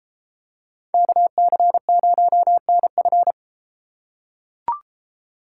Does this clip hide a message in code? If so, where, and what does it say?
Yes: Morse code from 0.9 to 3.3 s, heard as a keyed tone, about 32 wpm, 700 Hz. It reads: KC0NF